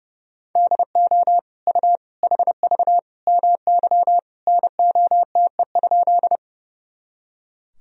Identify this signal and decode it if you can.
Morse code, heard as a keyed tone, about 31 wpm, 700 Hz. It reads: DO U HV MY NOTE?
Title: Sentences from Top 500 Words and Abbreviations 30wpm